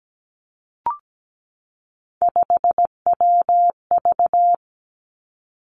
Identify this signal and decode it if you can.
Morse code, heard as a keyed tone, about 17 wpm, 700 Hz. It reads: 5WV
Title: Three Challenging Letter-Number Combinations 17wpm